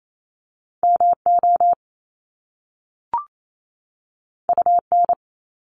Morse code, heard as a keyed tone, about 28 wpm, 700 Hz. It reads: MO UN
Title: Top Two Letter Combinations in Words 28wpm